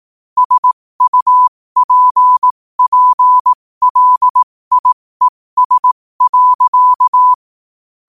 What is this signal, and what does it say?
Morse code, heard as a keyed tone, about 18 wpm, spaced as for 15 wpm, 1000 Hz. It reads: SUPPLIES.
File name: qotdfiles / 15wpm